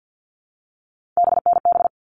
Morse code, heard as a keyed tone, about 50 wpm, 700 Hz. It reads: BNB
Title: All Cut Numbers - 3-Digit 50wpm